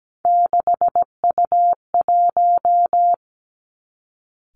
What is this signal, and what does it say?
Morse code, heard as a keyed tone, about 17 wpm, 700 Hz. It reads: 6U1
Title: Three Challenging Letter-Number Combinations 17wpm